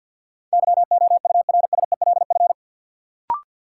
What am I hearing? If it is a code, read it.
Morse code, heard as a keyed tone, about 50 wpm, 700 Hz. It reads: YOURSELF
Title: Sentences from Top 600 Words - New Word Drill - 50wpm